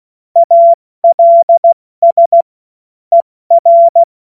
Morse code, heard as a keyed tone, about 16 wpm, spaced as for 14 wpm, 680 Hz. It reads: ALS ER
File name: Morsekurs / Witze / Witz 16 Januar 2024 70 F80